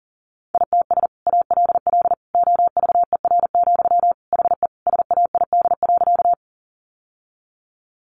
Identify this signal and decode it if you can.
Morse code, heard as a keyed tone, about 40 wpm, 700 Hz. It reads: ITS ALL OVER, HE SAID.
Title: Sentences from Top 600 Words 40wpm